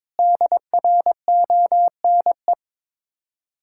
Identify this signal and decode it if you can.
Morse code, heard as a keyed tone, about 21 wpm, 690 Hz. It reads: DRONE
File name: Difficult Words with Hints - 22wpm